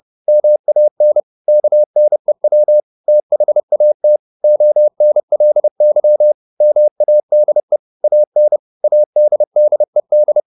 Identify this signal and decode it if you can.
Morse code, heard as a keyed tone, about 30 wpm, 600 Hz. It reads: MAN KNEW THAT ONLY MADE AN ADDED
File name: the old man and the sea / the old man and the sea 1 30wpm